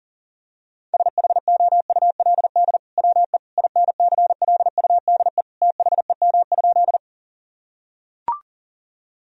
Morse code, heard as a keyed tone, about 37 wpm, 700 Hz. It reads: SHOULD WE INCLUDE THEM?
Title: Sentences from Top 1000 Words - Review Part 3 - 40wpm